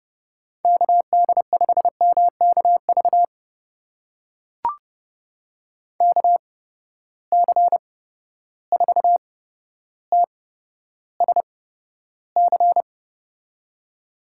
Morse code, heard as a keyed tone, about 31 wpm, spaced as for 13 wpm, 700 Hz. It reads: KD5MKV KC4TSC
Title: USA Call Signs - 6 Characters - 8x Character Spacing - 30wpm